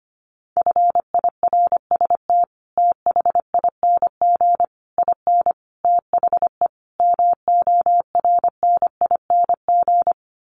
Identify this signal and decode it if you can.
Morse code, heard as a keyed tone, about 25 wpm, 700 Hz. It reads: FIRST THING IN THE MORNING
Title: Sets of 5 Words Encore2 25wpm